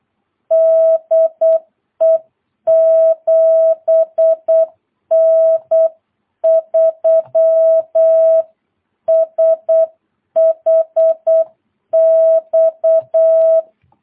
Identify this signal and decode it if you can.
Morse code, heard as a keyed tone, about 8 wpm, 660 Hz. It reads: DE 7N3SHX
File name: jh8cbh / 2022cw